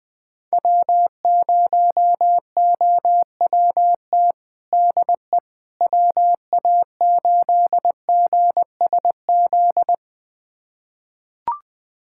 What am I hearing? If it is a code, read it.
Morse code, heard as a keyed tone, about 20 wpm, 700 Hz. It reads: W0OWT DE WA8GSZ
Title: USA Callsign DE Callsign 20wpm